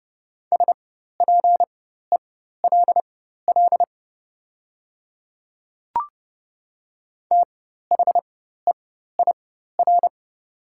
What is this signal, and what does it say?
Morse code, heard as a keyed tone, about 31 wpm, spaced as for 14 wpm, 700 Hz. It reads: SPELL THEIR
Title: 4x Character Spacing - Top 300 Words - 30wpm